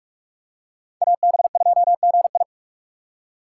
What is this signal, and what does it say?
Morse code, heard as a keyed tone, about 41 wpm, 700 Hz. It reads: AB2GI
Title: K1USN SST QSO Practice - 45wpm